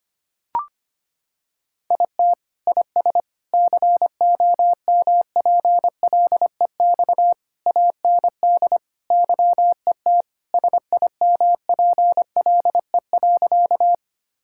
Morse code, heard as a keyed tone, about 25 wpm, 700 Hz. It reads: IT IS COMPLEX AND YET SIMPLE.
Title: Sentences from Top 1000 Words - Review Part 6 - 25wpm